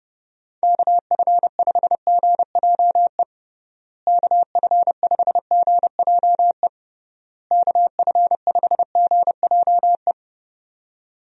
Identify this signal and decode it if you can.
Morse code, heard as a keyed tone, about 30 wpm, 700 Hz. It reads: KF5GJE KF5GJE KF5GJE
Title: USA Call Signs - 6 Characters - Repeated 3x - 30wpm